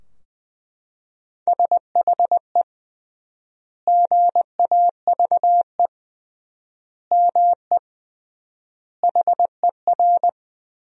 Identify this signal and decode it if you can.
Morse code, heard as a keyed tone, about 20 wpm, 700 Hz. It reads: SHE GAVE ME HER
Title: 3x Word-Spacing - Sentences from Top 500 Words - 20wpm